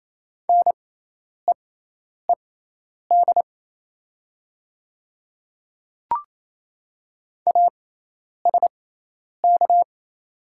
Morse code, heard as a keyed tone, about 27 wpm, spaced as for 10 wpm, 700 Hz. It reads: NEED ASK
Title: 6x Character Spacing - Top 200 Words - 28wpm